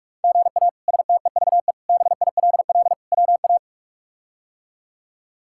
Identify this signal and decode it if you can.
Morse code, heard as a keyed tone, about 46 wpm, 700 Hz. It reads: GA STEVE BILL WA
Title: K1USN SST QSO Practice - 45wpm